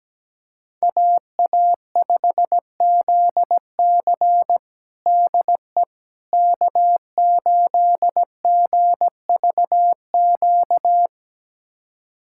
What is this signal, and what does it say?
Morse code, heard as a keyed tone, about 17 wpm, 700 Hz. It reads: AA5ZC DE K8GVQ
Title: USA Callsign DE Callsign - 5 Characters - 17wpm